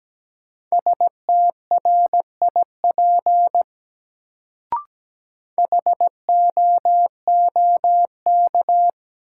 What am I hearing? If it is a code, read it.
Morse code, heard as a keyed tone, about 17 wpm, 700 Hz. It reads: STRIP HOOK
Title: Sentences from Top 1600 Words - New Word Drill Alt - 17wpm